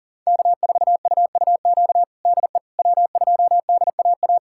Morse code, heard as a keyed tone, about 40 wpm, 700 Hz. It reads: K4UUQ DE W2DAA
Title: USA Callsign DE Callsign - 5 Characters - 40wpm